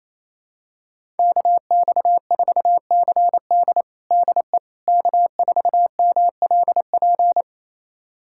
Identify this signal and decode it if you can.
Morse code, heard as a keyed tone, about 27 wpm, 700 Hz. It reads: KX4CD DE K4MLP
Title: USA Callsign DE Callsign - 5 Characters - 28wpm